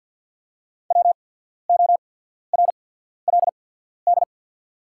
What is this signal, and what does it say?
Morse code, heard as a keyed tone, about 48 wpm, spaced as for 14 wpm, 690 Hz. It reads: WORLD
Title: 8x Character Spacing - Top 200 Words - 50wpm